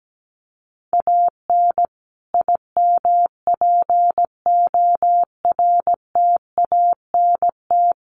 Morse code, heard as a keyed tone, about 17 wpm, 700 Hz. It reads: AN IMPORTANT
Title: Sets of 5 Words Encore2 17wpm